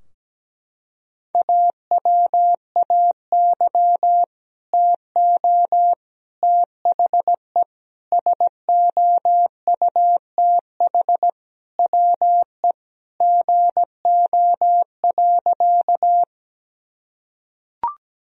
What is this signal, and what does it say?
Morse code, heard as a keyed tone, about 17 wpm, 700 Hz. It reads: AWAY TO THE SOUTH WE GO.
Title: Sentences from Top 1000 Words - Review Part 2 - 17wpm